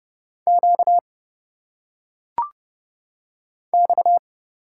Morse code, heard as a keyed tone, about 30 wpm, 700 Hz. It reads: Q X
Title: Single Letters - Phonetic Pronunciation 30wpm